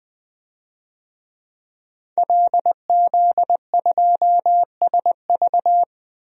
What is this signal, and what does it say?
Morse code, heard as a keyed tone, about 21 wpm, 700 Hz. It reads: LZ2SV
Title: DXCC-100 Call Signs - 5 Characters - 20wpm